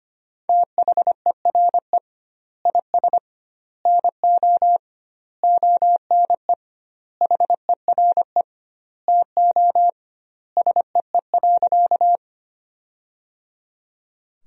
Morse code, heard as a keyed tone, about 25 wpm, 700 Hz. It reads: THERE IS NO ONE HERE TO SEE.
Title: 2x Word-Spacing - Sentences from Top 200 Words - 25wpm